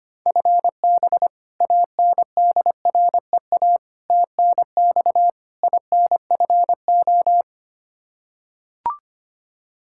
Morse code, heard as a keyed tone, about 25 wpm, 700 Hz. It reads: FB ANDREA TNX INFO
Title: Practice QSO - Exchange 3 - More Info - 25wpm